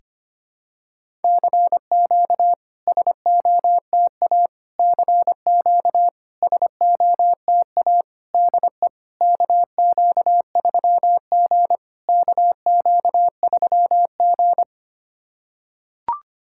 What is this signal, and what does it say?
Morse code, heard as a keyed tone, about 25 wpm, 690 Hz. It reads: CQ SOTA CQ SOTA DE KQ3G KQ3G
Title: SOTA QSO Practice 25wpm